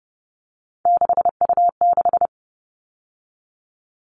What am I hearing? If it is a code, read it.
Morse code, heard as a keyed tone, about 30 wpm, 700 Hz. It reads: 6U6
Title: All Cut Numbers - 3-Digit 30wpm